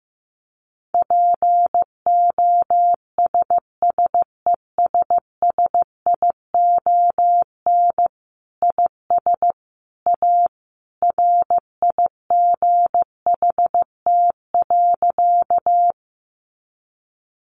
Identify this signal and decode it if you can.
Morse code, heard as a keyed tone, about 15 wpm, 700 Hz. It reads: POSSESSION IS A RIGHT.
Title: Sentences from Top 1400 Words 15wpm